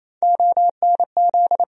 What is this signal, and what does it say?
Morse code, heard as a keyed tone, about 28 wpm, 700 Hz. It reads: ONZ
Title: USA Call Sign Postfixes - 3 Letters 28wpm